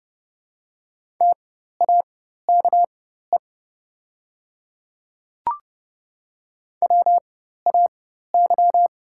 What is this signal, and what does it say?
Morse code, heard as a keyed tone, about 31 wpm, spaced as for 14 wpm, 690 Hz. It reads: TAKE WAY
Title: 4x Character Spacing - Top 200 Words - 30wpm